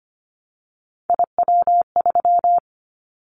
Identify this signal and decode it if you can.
Morse code, heard as a keyed tone, about 25 wpm, 700 Hz. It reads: IW3
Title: Callsign Prefixes for Top 25 DXCC - Repeated 3x - 25wpm